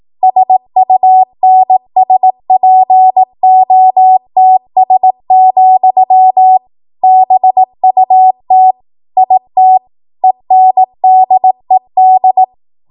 Morse code, heard as a keyed tone, about 18 wpm, 750 Hz. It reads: SUNSPOTS, BUT IT ENDED
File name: Archive / 18 WPM / 18WPM